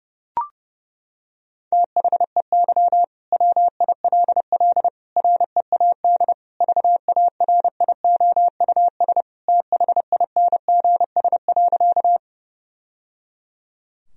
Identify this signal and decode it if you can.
Morse code, heard as a keyed tone, about 30 wpm, 700 Hz. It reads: THEY WILL READ VARIOUS THINGS.
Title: Sentences from Top 1000 Words - Review Part 4 - 30wpm